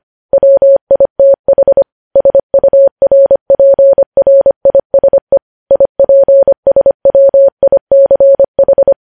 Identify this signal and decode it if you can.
Morse code, heard as a keyed tone, about 25 wpm, 550 Hz. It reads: WITH SURPRISE IPSWICH
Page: News Headlines in Morse Code at 25 WPM